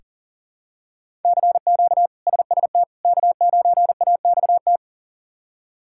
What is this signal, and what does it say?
Morse code, heard as a keyed tone, about 40 wpm, 700 Hz. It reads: CQ SST K9AXT
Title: K1USN SST QSO Practice - 40wpm